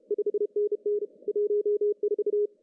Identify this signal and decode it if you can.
Morse code, heard as a keyed tone, about 35 wpm, 410 Hz. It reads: 5NN 14
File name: CW / cqww-cw-k5tr